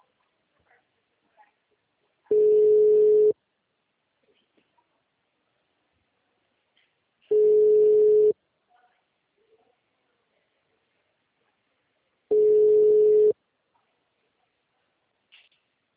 Ringback tone (Europe, 425 Hz).
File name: out- -2003- - - .48185